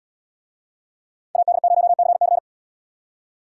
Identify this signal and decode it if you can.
Morse code, heard as a keyed tone, about 74 wpm, 700 Hz. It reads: AD0CC